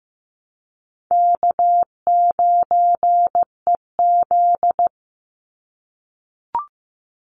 Morse code, heard as a keyed tone, about 15 wpm, 700 Hz. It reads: K9EZ
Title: NA SOTA Chaser Callsigns 15wpm